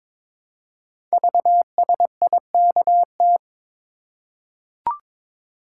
Morse code, heard as a keyed tone, about 22 wpm, 700 Hz. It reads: VSIKT
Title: Code Groups - 5 Letters - 22wpm